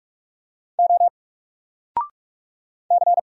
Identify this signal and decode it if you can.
Morse code, heard as a keyed tone, about 44 wpm, 700 Hz. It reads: O C
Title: Single Letter-Number Alt 45wpm